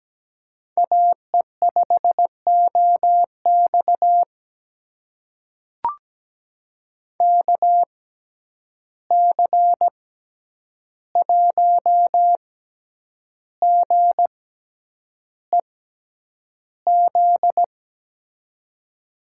Morse code, heard as a keyed tone, about 17 wpm, spaced as for 8 wpm, 700 Hz. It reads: AE5OX KC1GEZ
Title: USA Call Signs - 6x Character Spacing - 17wpm